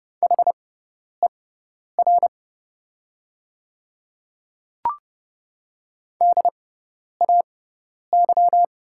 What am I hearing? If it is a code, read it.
Morse code, heard as a keyed tone, about 30 wpm, spaced as for 10 wpm, 700 Hz. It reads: HER DAY
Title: 6x Character Spacing - Top 200 Words - 30wpm